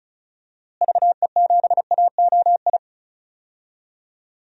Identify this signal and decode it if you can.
Morse code, heard as a keyed tone, about 36 wpm, 700 Hz. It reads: VE7AOI
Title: DXCC-100 Call Signs - 6 Characters - Repeated 3x - 35wpm